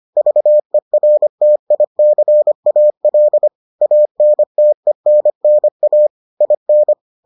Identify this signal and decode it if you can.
Morse code, heard as a keyed tone, about 25 wpm, 600 Hz. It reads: VERTICAL ANTENNA IN